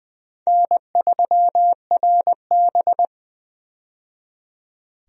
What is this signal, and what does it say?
Morse code, heard as a keyed tone, about 20 wpm, 700 Hz. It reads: N3RB
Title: USA Call Signs - 4 Characters - 20wpm